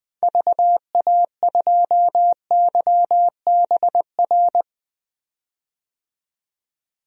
Morse code, heard as a keyed tone, about 20 wpm, 700 Hz. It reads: VA2YBR
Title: DXCC-100 Call Signs - 6 Characters - 20wpm